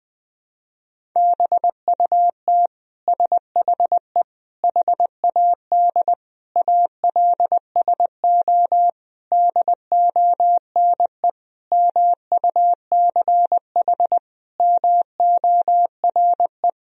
Morse code, heard as a keyed tone, about 20 wpm, 700 Hz. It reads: BUT SHE HAD ALSO DONE MUCH MORE
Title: Sentences from Top 1000 Words - Review Part 1 - 20wpm